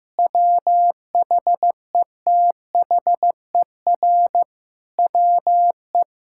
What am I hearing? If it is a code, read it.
Morse code, heard as a keyed tone, about 15 wpm, 700 Hz. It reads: WHETHER WE
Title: Sentences from Top 1000 Words - Review Part 6 - 15wpm